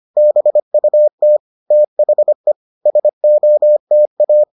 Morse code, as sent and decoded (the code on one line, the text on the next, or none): -... ..- - / - .... / ... --- - .-
BUT THE SOTA